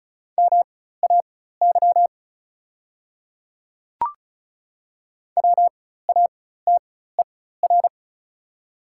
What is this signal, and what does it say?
Morse code, heard as a keyed tone, about 36 wpm, spaced as for 16 wpm, 690 Hz. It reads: MAY WATER
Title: 4x Character Spacing - Top 100 Words - 35wpm